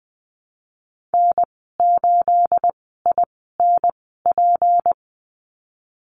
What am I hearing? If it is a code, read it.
Morse code, heard as a keyed tone, about 20 wpm, spaced as for 15 wpm, 700 Hz. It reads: N8INP
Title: 231 Lesson 37g - Letter x - Call Signs 20wpm